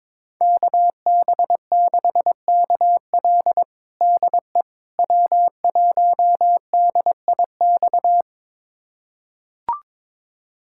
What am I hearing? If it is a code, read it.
Morse code, heard as a keyed tone, about 22 wpm, 700 Hz. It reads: KB6KL DE W1DIX
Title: USA Callsign DE Callsign 22wpm